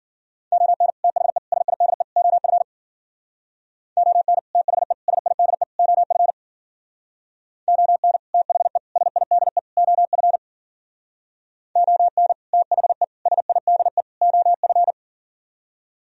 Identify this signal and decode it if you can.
Morse code, heard as a keyed tone, about 51 wpm, 700 Hz. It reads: ON THE SIDE OF ON THE SIDE OF ON THE SIDE OF ON THE SIDE OF